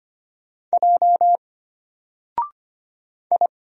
Morse code, heard as a keyed tone, about 25 wpm, 700 Hz. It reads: J I
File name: Single-Letter-Number-25wpm